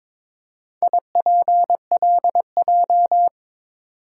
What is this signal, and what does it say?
Morse code, heard as a keyed tone, about 22 wpm, 700 Hz. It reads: IPLJ
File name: practice / 22wpm / Four Letter-Number Combinations - 22wpm